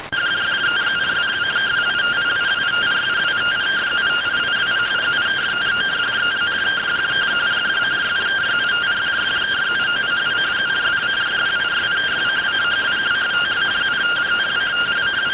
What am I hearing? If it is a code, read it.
Two-tone teleprinter signal, about 170 Hz shift.